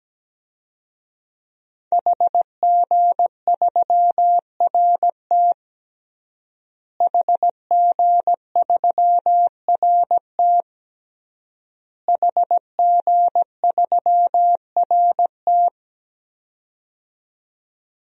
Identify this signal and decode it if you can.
Morse code, heard as a keyed tone, about 17 wpm, 700 Hz. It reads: HG3RT HG3RT HG3RT
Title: DXCC-100 Call Signs - 5 Characters - Repeated 3x - 17wpm